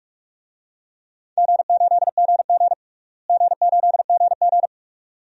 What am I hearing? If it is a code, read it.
Morse code, heard as a keyed tone, about 46 wpm, 700 Hz. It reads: G8GG G8GG
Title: DXCC-100 Call Signs - 4 Characters - Repeated 2x - 45wpm